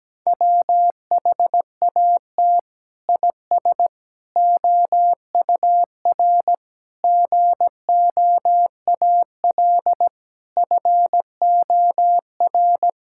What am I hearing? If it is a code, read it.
Morse code, heard as a keyed tone, about 17 wpm, 700 Hz. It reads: WHAT IS OUR GOAL FOR
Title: Sentences from Top 1000 Words - Review Part 5 - 17wpm